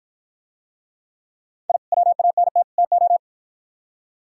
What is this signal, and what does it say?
Morse code, heard as a keyed tone, about 51 wpm, 700 Hz. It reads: I WANT TO